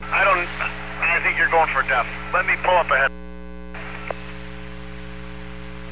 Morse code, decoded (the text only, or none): none